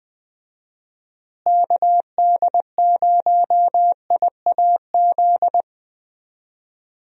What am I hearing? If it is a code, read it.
Morse code, heard as a keyed tone, about 20 wpm, 700 Hz. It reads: KD0IAZ
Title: USA Call Signs - 6 Characters - Repeated 2x - 20wpm